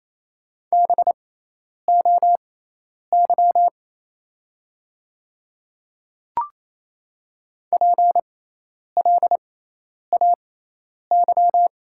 Morse code, heard as a keyed tone, about 27 wpm, spaced as for 10 wpm, 700 Hz. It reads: BOY PLAY
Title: 6x Character Spacing - Top 500 Words - 28wpm